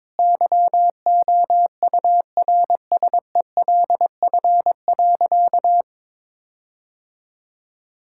Morse code, heard as a keyed tone, about 22 wpm, 700 Hz. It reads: YOURSELF.
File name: practice / 22wpm / Sentences with 8-Word Phrases - 22wpm